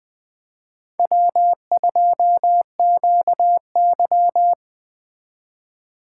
Morse code, heard as a keyed tone, about 20 wpm, 700 Hz. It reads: W2QY
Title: Speed Racing - USA 1x2 Call Signs 17wpm